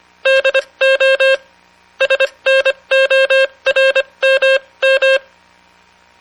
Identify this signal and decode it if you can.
Morse code, heard as a keyed tone, about 24 wpm, spaced as for 20 wpm, 540 Hz. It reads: DO SNORMM